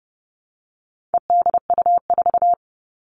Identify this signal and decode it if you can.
Morse code, heard as a keyed tone, about 30 wpm, 700 Hz. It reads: EDU4
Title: All Cut Numbers - 4-Digit 30wpm